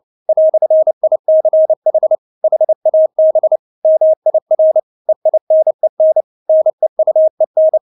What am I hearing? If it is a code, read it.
Morse code, heard as a keyed tone, about 29 wpm, 630 Hz. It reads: "ICH HAB MIR EINEN NEUEN
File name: Witz10 29wpm